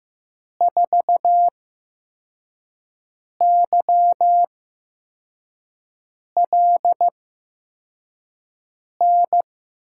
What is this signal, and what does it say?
Morse code, heard as a keyed tone, about 15 wpm, spaced as for 4 wpm, 700 Hz. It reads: 4YLN
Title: USA Call Signs - 6 Characters - 8x Character Spacing - 15wpm